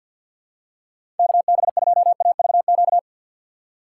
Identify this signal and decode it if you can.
Morse code, heard as a keyed tone, about 47 wpm, 700 Hz. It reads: KB2AVQ
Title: USA Call Signs - 6 Characters - 3x Character Spacing - 50wpm